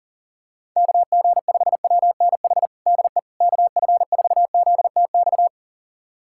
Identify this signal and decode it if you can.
Morse code, heard as a keyed tone, about 40 wpm, 700 Hz. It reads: KG5WNH DE KF4ZTX
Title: USA Callsign DE Callsign 40wpm